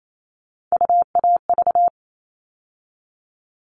Morse code, heard as a keyed tone, about 28 wpm, 700 Hz. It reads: UAV
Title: All Cut Numbers - 3-Digit 28wpm